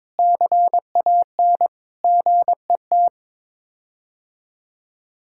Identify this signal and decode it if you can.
Morse code, heard as a keyed tone, about 22 wpm, 700 Hz. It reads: CAN GET
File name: practice / 22wpm / Sets of 8 Words - 22wpm